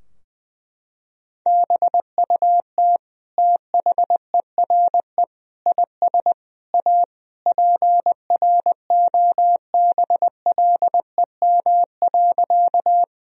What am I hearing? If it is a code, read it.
Morse code, heard as a keyed tone, about 21 wpm, 700 Hz. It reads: BUT THERE IS A PROBLEM.
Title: Sentences from Top 1000 Words - Review Part 2 - 20wpm